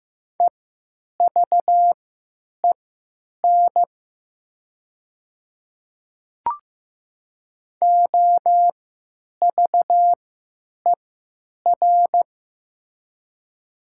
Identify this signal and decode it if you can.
Morse code, heard as a keyed tone, about 15 wpm, spaced as for 9 wpm, 700 Hz. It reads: EVEN OVER
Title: 3x Character Spacing - Top 500 Words - 15wpm